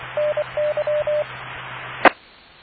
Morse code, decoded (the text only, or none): NY E